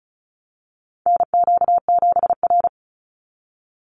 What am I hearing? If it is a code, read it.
Morse code, heard as a keyed tone, about 34 wpm, 700 Hz. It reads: NQ7R
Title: NA SOTA Chaser Callsigns 35wpm